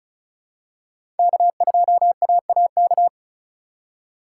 Morse code, heard as a keyed tone, about 36 wpm, 700 Hz. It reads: K2AAK
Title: USA Call Signs - 5 Characters - 3x Character Spacing - 35wpm